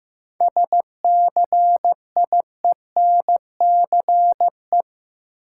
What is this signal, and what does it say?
Morse code, heard as a keyed tone, about 15 wpm, 700 Hz. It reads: SCIENCE
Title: Sentences from Top 1000 Words - Review Part 2 - New Word Drill - 15wpm